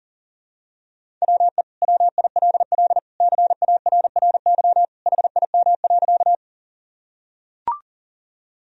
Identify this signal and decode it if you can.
Morse code, heard as a keyed tone, about 40 wpm, 700 Hz. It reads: WE WILL CARRY HIM.
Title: Sentences from Top 1000 Words - Review Part 1 - 40wpm